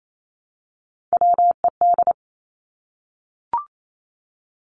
Morse code, heard as a keyed tone, about 28 wpm, 700 Hz. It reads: WED
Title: Three Challenging Letter-Number Combinations 28wpm